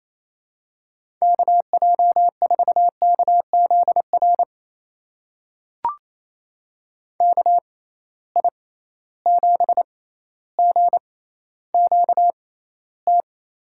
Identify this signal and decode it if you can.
Morse code, heard as a keyed tone, about 28 wpm, spaced as for 14 wpm, 700 Hz. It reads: KJ4KZR KI7GQT
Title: USA Call Signs - 6 Characters - 6x Character Spacing - 28wpm